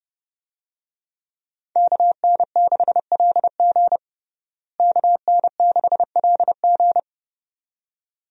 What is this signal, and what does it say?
Morse code, heard as a keyed tone, about 32 wpm, 700 Hz. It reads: KN6LG KN6LG
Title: USA Call Signs - 5 Characters - Repeated 2x - 30wpm